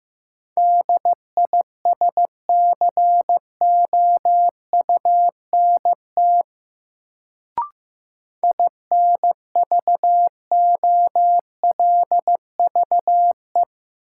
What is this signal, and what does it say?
Morse code, heard as a keyed tone, about 15 wpm, 700 Hz. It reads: DISCOUNT INVOLVE
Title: Sentences from Top 1200 Words - New Word Drill - 15wpm